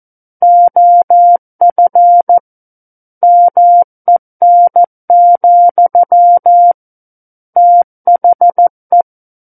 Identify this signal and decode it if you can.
Morse code, heard as a keyed tone, about 14 wpm, 700 Hz. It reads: OF MEN, THE
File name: Book One / 05-Chapter 5 - 14 WPM